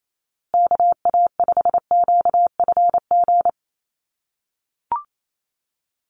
Morse code, heard as a keyed tone, about 28 wpm, 700 Hz. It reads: KA5QFG
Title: USA Call Signs Repeated 3x 28wpm